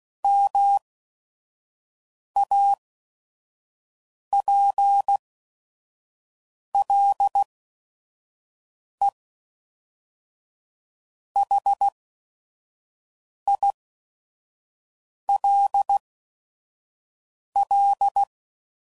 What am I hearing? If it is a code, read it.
Morse code, heard as a keyed tone, about 16 wpm, spaced as for 5 wpm, 800 Hz. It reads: MAPLEHILL
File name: CW Code Groups @5WPM / fromSM's / Cities